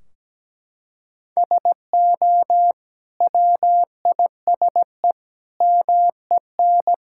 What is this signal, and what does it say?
Morse code, heard as a keyed tone, about 17 wpm, 700 Hz. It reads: SO WISE MEN
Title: Sentences from Top 1600 Words 17wpm